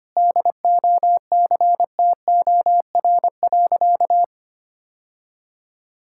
Morse code, heard as a keyed tone, about 25 wpm, 700 Hz. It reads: DOCTOR.
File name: practice / 25wpm / Sentences with 3-Word Phrases - 25wpm